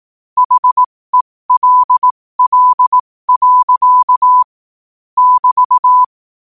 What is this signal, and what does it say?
Morse code, heard as a keyed tone, about 18 wpm, spaced as for 15 wpm, 1000 Hz. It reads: HELL. <BT>
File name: qotdfiles / 15wpm